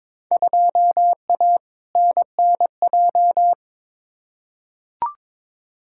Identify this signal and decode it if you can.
Morse code, heard as a keyed tone, about 22 wpm, 700 Hz. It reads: 2A NNJ
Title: Field Day Exchanges - 22wpm